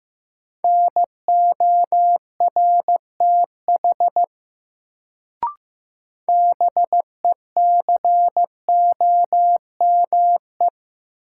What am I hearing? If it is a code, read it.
Morse code, heard as a keyed tone, about 15 wpm, 700 Hz. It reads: NORTH BECOME
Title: Sets of 5 Words Encore - New Word Drill - 15wpm